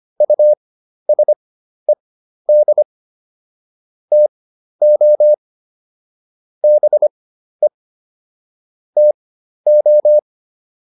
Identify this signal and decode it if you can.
Morse code, heard as a keyed tone, about 25 wpm, spaced as for 12 wpm, 600 Hz. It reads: USED TO BE TO